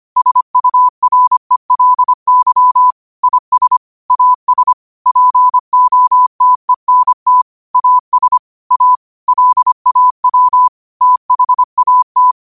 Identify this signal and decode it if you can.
Morse code, heard as a keyed tone, about 25 wpm, 1000 Hz. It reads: SURELY IS AS POTENT AS A LAW THAT
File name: qotdfiles / 25wpm